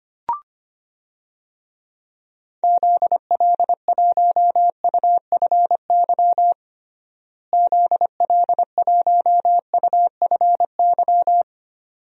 Morse code, heard as a keyed tone, about 25 wpm, 700 Hz. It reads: ZL1UFY ZL1UFY
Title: DXCC-100 Call Signs - 6 Characters - Repeated 2x - 25wpm